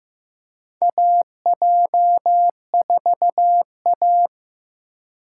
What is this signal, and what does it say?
Morse code, heard as a keyed tone, about 15 wpm, 700 Hz. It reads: AJ4A
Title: USA Call Signs - 4 Characters - 3x Character Spacing - 15wpm